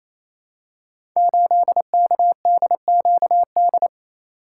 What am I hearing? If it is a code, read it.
Morse code, heard as a keyed tone, about 27 wpm, 700 Hz. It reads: 8KDQD
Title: Code Groups - 5 Letter:Numbers - 28wpm